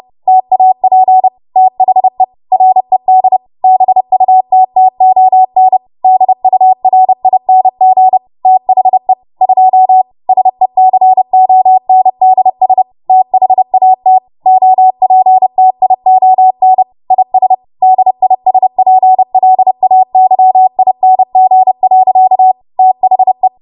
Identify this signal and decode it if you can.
Morse code, heard as a keyed tone, about 30 wpm, 750 Hz. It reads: TAP THE RED BUTTON DURING THE 2 SECONDS THAT OPTION IS DISPLAYING. THE
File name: Morse / 30WPM